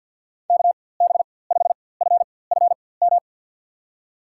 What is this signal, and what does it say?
Morse code, heard as a keyed tone, about 48 wpm, spaced as for 23 wpm, 690 Hz. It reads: KB5FFM
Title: USA Call Signs - 4x Character Spacing - 50wpm